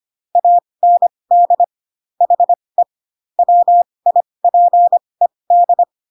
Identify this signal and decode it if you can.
Morse code, heard as a keyed tone, about 25 wpm, spaced as for 20 wpm, 700 Hz. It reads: AND HE WIPED